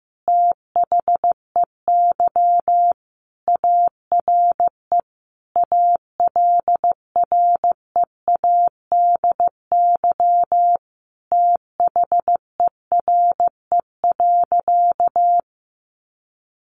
Morse code, heard as a keyed tone, about 15 wpm, 700 Hz. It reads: THEY ARE ALREADY THERE.
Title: Sentences from Top 700 Words 15wpm